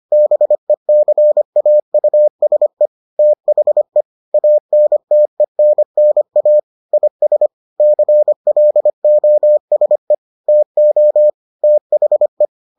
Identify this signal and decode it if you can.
Morse code, heard as a keyed tone, about 25 wpm, 600 Hz. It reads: BECAUSE THE ANTENNA IS CLOSE TO THE